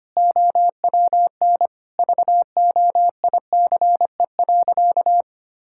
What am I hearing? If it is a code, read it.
Morse code, heard as a keyed tone, about 25 wpm, 700 Hz. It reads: OWN VOICE.
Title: Sentences from Top 1600 Words 25wpm